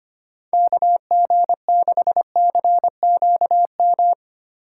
Morse code, heard as a keyed tone, about 25 wpm, 700 Hz. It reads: KG6CQM
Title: USA Call Signs - 6 Characters - 4x Character Spacing - 25wpm